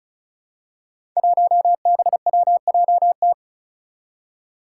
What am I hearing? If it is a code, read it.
Morse code, heard as a keyed tone, about 36 wpm, 700 Hz. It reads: 1BWJT
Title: Five Challenging Letter-Number Combinations 35wpm